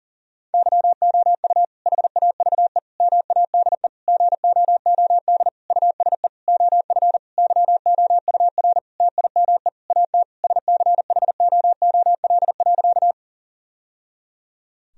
Morse code, heard as a keyed tone, about 40 wpm, 700 Hz. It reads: YOU HAVE MADE GOOD USE OF YOUR TIME AT SCHOOL.
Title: Sentences from Top 1000 Words - Review Part 1 - 40wpm